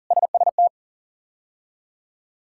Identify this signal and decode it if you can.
Morse code, heard as a keyed tone, about 40 wpm, 700 Hz. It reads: SST
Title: K1USN SST QSO Practice - 40wpm